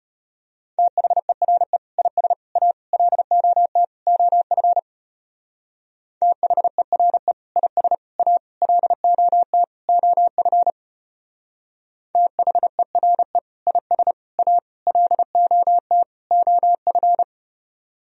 Morse code, heard as a keyed tone, about 33 wpm, 700 Hz. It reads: THERE IS A LOT OF THERE IS A LOT OF THERE IS A LOT OF